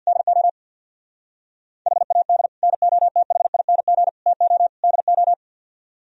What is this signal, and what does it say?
Morse code, heard as a keyed tone, about 60 wpm, 700 Hz. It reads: DO HAD NOTHING TO DO
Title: Speed-Racing - Sets of 4 Words 50wpm